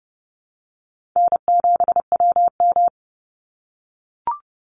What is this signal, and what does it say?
Morse code, heard as a keyed tone, about 30 wpm, 700 Hz. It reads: N7WM
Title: NA SOTA Chaser Callsigns 30wpm